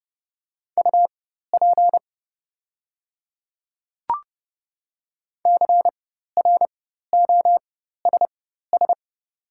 Morse code, heard as a keyed tone, about 30 wpm, spaced as for 14 wpm, 690 Hz. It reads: UP CROSS
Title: 4x Character Spacing - Top 500 Words - 30wpm